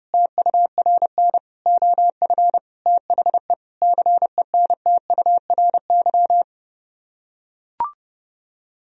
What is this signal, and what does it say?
Morse code, heard as a keyed tone, about 30 wpm, 700 Hz. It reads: TURN OF THE CENTURY